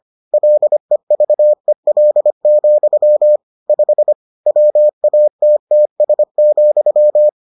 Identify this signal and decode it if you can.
Morse code, heard as a keyed tone, about 25 wpm, 600 Hz. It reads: LEVEL, 5 WATTS,